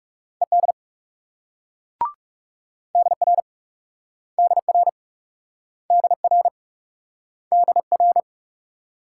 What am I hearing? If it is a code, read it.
Morse code, heard as a keyed tone, about 39 wpm, 700 Hz. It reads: ED DR DR DR DR